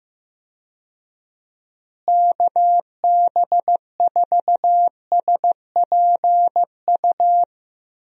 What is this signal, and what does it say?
Morse code, heard as a keyed tone, about 15 wpm, 700 Hz. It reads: KB4SPU
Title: USA Call Signs - 6 Characters - Repeated 3x - 15wpm